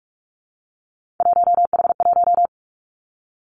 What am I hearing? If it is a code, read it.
Morse code, heard as a keyed tone, about 46 wpm, 700 Hz. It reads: 1H1
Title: Three Challenging Letter-Number Combinations 45wpm